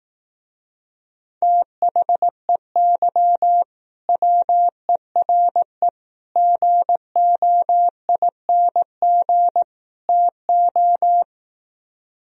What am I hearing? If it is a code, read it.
Morse code, heard as a keyed tone, about 18 wpm, 700 Hz. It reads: THEY WERE GOING TO
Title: Speed-Racing - Sets of 4 Words 15wpm